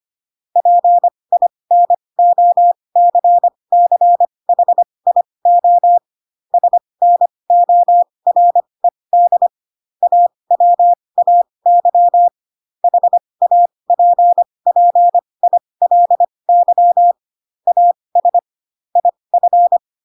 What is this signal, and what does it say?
Morse code, heard as a keyed tone, about 25 wpm, spaced as for 20 wpm, 700 Hz. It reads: PINOCCHIO SNORED AWAY HAPPILY AS IF
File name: Pinocchio / Pinocchio ch6 15 25 ik1hge